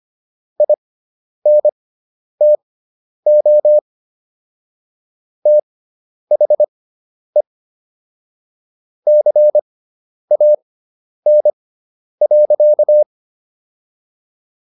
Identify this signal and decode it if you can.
Morse code, heard as a keyed tone, about 25 wpm, spaced as for 10 wpm, 600 Hz. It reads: INTO THE CAN.